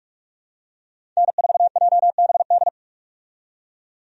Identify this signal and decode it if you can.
Morse code, heard as a keyed tone, about 44 wpm, 700 Hz. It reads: N4JBD
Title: USA Call Signs - 5 Characters - 45wpm